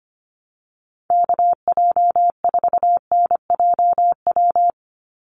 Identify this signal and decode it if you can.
Morse code, heard as a keyed tone, about 25 wpm, 700 Hz. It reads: KJ4NJW
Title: USA Call Signs 25wpm